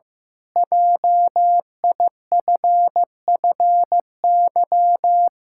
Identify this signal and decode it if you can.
Morse code, heard as a keyed tone, about 15 wpm, 700 Hz. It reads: JIFFY
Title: Pangram Challenge v2 - 15wpm